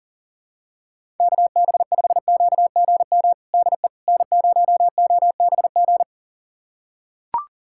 Morse code, heard as a keyed tone, about 37 wpm, 700 Hz. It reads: KB5QGM DE N0OBG
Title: USA Callsign DE Callsign 40wpm